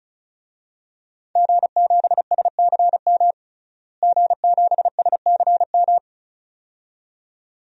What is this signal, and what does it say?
Morse code, heard as a keyed tone, about 34 wpm, 700 Hz. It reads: G7SCM G7SCM
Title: DXCC-100 Call Signs - 5 Characters - Repeated 2x - 35wpm